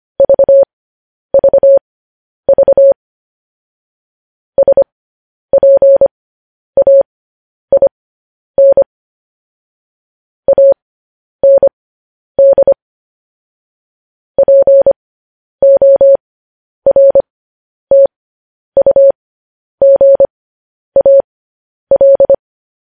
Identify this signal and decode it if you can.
Morse code, heard as a keyed tone, about 25 wpm, spaced as for 10 wpm, 550 Hz. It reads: VVV SPAIN AND PORTUGAL
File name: podcast / News headlines-1x-25x10